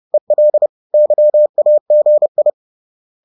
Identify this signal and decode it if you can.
Morse code, heard as a keyed tone, about 30 wpm, 600 Hz. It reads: EL YAGI
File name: 30wpm-4